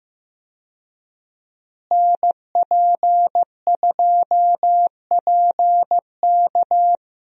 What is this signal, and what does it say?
Morse code, heard as a keyed tone, about 15 wpm, 700 Hz. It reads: NP2PK
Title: USA Call Signs - 5 Characters - Repeated 2x - 15wpm